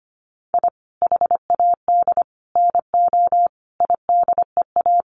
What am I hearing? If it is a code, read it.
Morse code, heard as a keyed tone, about 25 wpm, 700 Hz. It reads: I HAD NO IDEA
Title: Sets of 5 Words Encore 25wpm